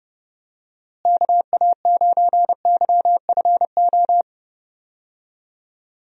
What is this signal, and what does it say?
Morse code, heard as a keyed tone, about 31 wpm, 700 Hz. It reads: KA9YFO
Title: USA Call Signs - 6 Characters - Repeated 2x - 30wpm